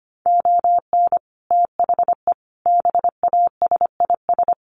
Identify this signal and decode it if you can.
Morse code, heard as a keyed tone, about 25 wpm, 700 Hz. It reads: ON THE BASIS
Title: Sets of 5 Words Encore 25wpm